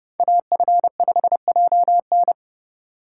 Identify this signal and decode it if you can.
Morse code, heard as a keyed tone, about 29 wpm, 700 Hz. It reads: AF5JN
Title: USA Call Signs - 5 Characters - 3x Character Spacing - 30wpm